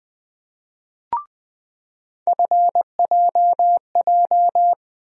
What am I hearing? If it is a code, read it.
Morse code, heard as a keyed tone, about 20 wpm, 700 Hz. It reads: FJJ
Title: USA Call Sign Postfixes - 3 Letters 20wpm